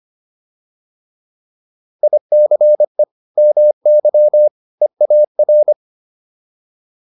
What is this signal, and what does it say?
Morse code, heard as a keyed tone, about 25 wpm, 600 Hz. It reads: ICE MY EAR